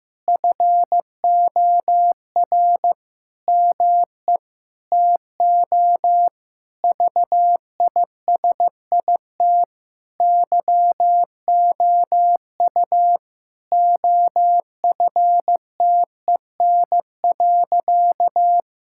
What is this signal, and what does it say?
Morse code, heard as a keyed tone, about 16 wpm, 700 Hz. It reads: FOR ME TO VISIT YOU OFTEN.
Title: Sentences from Top 1000 Words - Review Part 6 - 15wpm